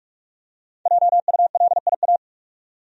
Morse code, heard as a keyed tone, about 45 wpm, 700 Hz. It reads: JULIA
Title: USA Common Names - 45wpm